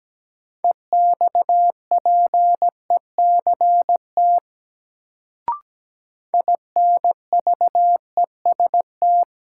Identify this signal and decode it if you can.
Morse code, heard as a keyed tone, about 17 wpm, 700 Hz. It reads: EXPECT INVEST
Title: Sentences from Top 1100 Words - New Word Drill - 17wpm